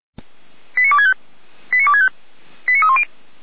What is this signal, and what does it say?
Five-tone selective call (ZVEI), heard as quick runs of five tones, tones of about 70 ms. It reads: 89265 89265 89310